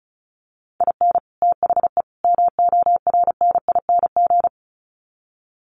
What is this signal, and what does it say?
Morse code, heard as a keyed tone, about 34 wpm, 700 Hz. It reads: IN THE MORNING
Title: Sets of 3 Words 35wpm